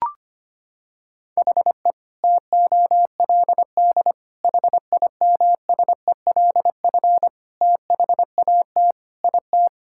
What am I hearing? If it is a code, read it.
Morse code, heard as a keyed tone, about 25 wpm, 700 Hz. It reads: HE TOLD HIMSELF THAT IT
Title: Sentences from Top 1600 Words 25wpm